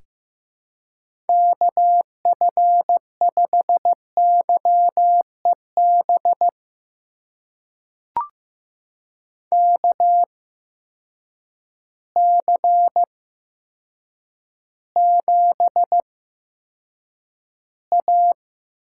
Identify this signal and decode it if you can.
Morse code, heard as a keyed tone, about 15 wpm, spaced as for 8 wpm, 700 Hz. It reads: KF5YEB KC7A
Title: USA Call Signs - 6 Characters - 8x Character Spacing - 15wpm